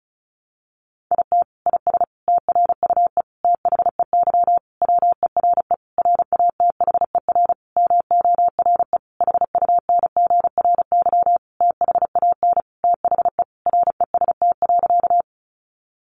Morse code, heard as a keyed tone, about 34 wpm, 700 Hz. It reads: IT IS TRUE THEY WERE RATHER MORE HUNGRY THAN THE REST.
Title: Sentences from Top 1400 Words 35wpm